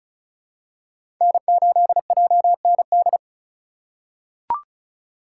Morse code, heard as a keyed tone, about 36 wpm, 700 Hz. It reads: N8JND
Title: USA Call Signs - 5 Characters - 2x Character Spacing - 35wpm